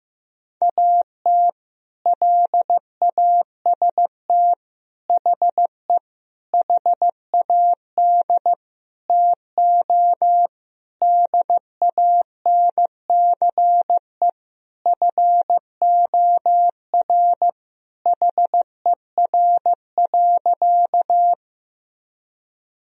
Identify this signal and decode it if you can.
Morse code, heard as a keyed tone, about 16 wpm, 700 Hz. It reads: AT LAST HE HAD TO DANCE FOR HER.
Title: Sentences from Top 1000 Words - Review Part 4 - 15wpm